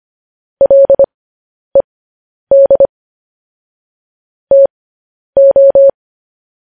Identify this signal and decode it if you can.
Morse code, heard as a keyed tone, about 25 wpm, spaced as for 10 wpm, 550 Hz. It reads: LED TO
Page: News Headlines in Morse Code at 10 WPM